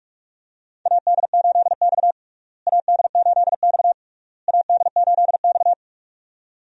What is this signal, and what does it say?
Morse code, heard as a keyed tone, about 43 wpm, 700 Hz. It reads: AD8X AD8X AD8X
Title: USA Call Signs - 4 Characters - Repeated 3x - 45wpm